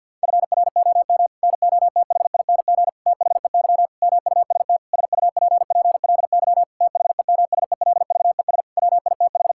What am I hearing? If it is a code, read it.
Morse code, heard as a keyed tone, about 52 wpm, 700 Hz. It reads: FROM NOTHING THEY MUST SUPPLY THEMSELVES WITH
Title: Sentences from Top 1000 Words - Review Part 5 - 50wpm